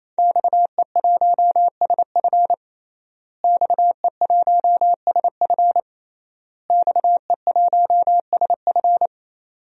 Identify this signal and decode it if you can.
Morse code, heard as a keyed tone, about 28 wpm, 700 Hz. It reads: XE1SF XE1SF XE1SF
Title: DXCC-100 Call Signs - 5 Characters - Repeated 3x - 28wpm